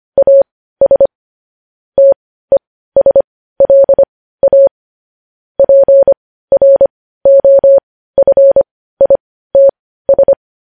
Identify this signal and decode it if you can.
Morse code, heard as a keyed tone, about 25 wpm, spaced as for 15 wpm, 550 Hz. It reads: AS TESLA PROFITS
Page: News Headlines in Morse Code at 15 WPM